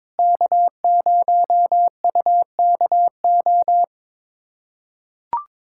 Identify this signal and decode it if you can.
Morse code, heard as a keyed tone, about 22 wpm, 700 Hz. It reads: K0UKO
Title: USA Callsign DE Callsign - 5 Characters - 22wpm